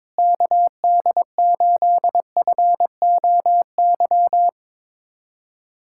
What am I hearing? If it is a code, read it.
Morse code, heard as a keyed tone, about 22 wpm, 700 Hz. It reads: KD8FOY
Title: USA Callsign DE Callsign 22wpm